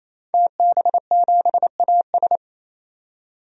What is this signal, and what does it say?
Morse code, heard as a keyed tone, about 27 wpm, 700 Hz. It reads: TB7AS
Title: DXCC-100 Call Signs - 5 Characters - 28wpm